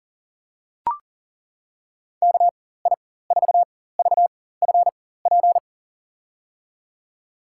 Morse code, heard as a keyed tone, about 40 wpm, spaced as for 19 wpm, 700 Hz. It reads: KI4VFP
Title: USA Call Signs - 6 Characters - 4x Character Spacing - 40wpm